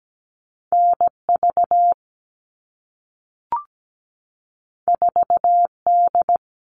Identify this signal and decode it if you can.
Morse code, heard as a keyed tone, about 17 wpm, 700 Hz. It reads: NV 4D
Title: All Cut Numbers - 2-Digit 17wpm